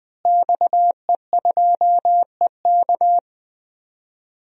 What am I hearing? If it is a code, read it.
Morse code, heard as a keyed tone, about 20 wpm, 700 Hz. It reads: XE2EK
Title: DXCC-100 Call Signs - 5 Characters - Repeated 3x - 20wpm